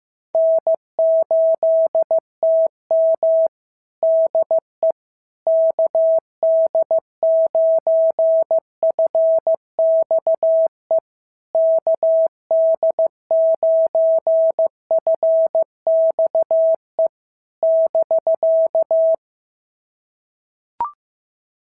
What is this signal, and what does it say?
Morse code, heard as a keyed tone, about 15 wpm, 650 Hz. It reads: N8TM DE KD9FXE KD9FXE <BK>
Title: SOTA QSO Practice 15wpm